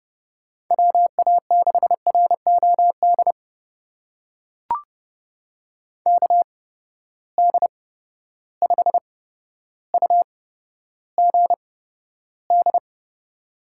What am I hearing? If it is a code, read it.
Morse code, heard as a keyed tone, about 31 wpm, spaced as for 13 wpm, 700 Hz. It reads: WA6ROD KD5UGD
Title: USA Call Signs - 6 Characters - 8x Character Spacing - 30wpm